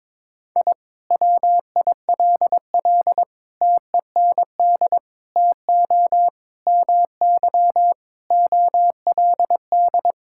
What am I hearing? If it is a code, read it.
Morse code, heard as a keyed tone, about 22 wpm, 700 Hz. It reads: I WILL TEND TO MY OLD
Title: Sentences from Top 1000 Words - Review Part 5 - 22wpm